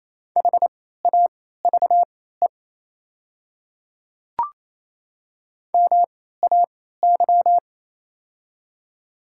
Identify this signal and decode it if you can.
Morse code, heard as a keyed tone, about 28 wpm, spaced as for 16 wpm, 700 Hz. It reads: HAVE MAY
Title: 3x Character Spacing - Top 200 Words - 28wpm